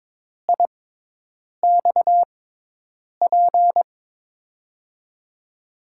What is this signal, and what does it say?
Morse code, heard as a keyed tone, about 22 wpm, spaced as for 8 wpm, 700 Hz. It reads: IXP
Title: USA Call Signs - 6 Characters - 6x Character Spacing - 22wpm